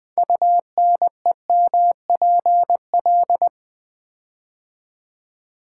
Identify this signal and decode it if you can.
Morse code, heard as a keyed tone, about 20 wpm, 700 Hz. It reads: UNEMPL
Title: Build A Word - 6 Letters At A Time - 12-Letter Words - 20wpm